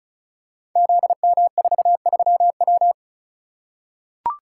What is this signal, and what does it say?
Morse code, heard as a keyed tone, about 34 wpm, 700 Hz. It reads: ZM43W
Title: Code Groups - 5 Letter:Numbers - 35wpm